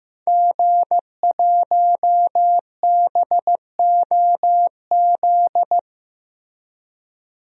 Morse code, heard as a keyed tone, about 15 wpm, 700 Hz. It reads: G1BOZ
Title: DXCC-100 Call Signs - 5 Characters - Repeated 3x - 15wpm